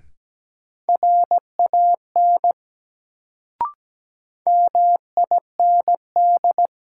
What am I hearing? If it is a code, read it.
Morse code, heard as a keyed tone, about 17 wpm, 700 Hz. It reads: RAN MIND
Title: Sentences from Top 500 Words - New Word Drill - 17wpm